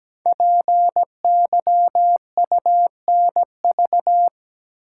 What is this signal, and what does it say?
Morse code, heard as a keyed tone, about 17 wpm, 700 Hz. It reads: PYUNV
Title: Code Groups - 5 Letters - 17wpm